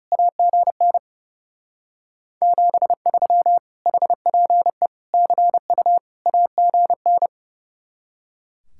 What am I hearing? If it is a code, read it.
Morse code, heard as a keyed tone, about 38 wpm, 700 Hz. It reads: AGN 73 HPE CU AGN